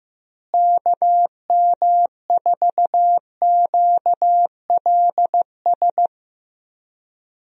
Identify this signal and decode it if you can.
Morse code, heard as a keyed tone, about 16 wpm, 700 Hz. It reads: KM4QLS
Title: USA Call Signs - 6 Characters - Repeated 2x - 15wpm